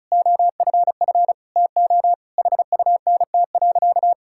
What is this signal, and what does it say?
Morse code, heard as a keyed tone, about 36 wpm, 700 Hz. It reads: OFF TO HUNT.
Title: Sentences from Top 1600 Words 35wpm